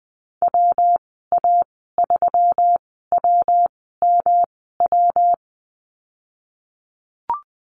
Morse code, heard as a keyed tone, about 20 wpm, spaced as for 14 wpm, 700 Hz. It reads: WA3WMW
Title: 122 Lesson 20g - Number 3 - Call Signs 20wpm